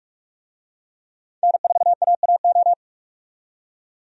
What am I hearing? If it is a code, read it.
Morse code, heard as a keyed tone, about 47 wpm, 700 Hz. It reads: N4AAO
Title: USA Call Signs - 5 Characters - 45wpm